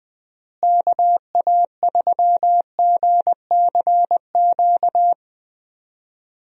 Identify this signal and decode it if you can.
Morse code, heard as a keyed tone, about 21 wpm, 700 Hz. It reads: KA3GCQ
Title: USA Call Signs - 6 Characters - Repeated 3x - 20wpm